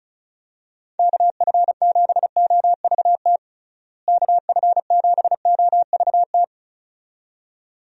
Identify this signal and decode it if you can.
Morse code, heard as a keyed tone, about 36 wpm, 700 Hz. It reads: KF7OVT KF7OVT
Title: USA Call Signs - 6 Characters - Repeated 2x - 35wpm